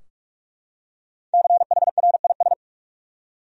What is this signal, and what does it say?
Morse code, heard as a keyed tone, about 44 wpm, 700 Hz. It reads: CHRIS
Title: USA Common Names - Max 5 Letters - 45wpm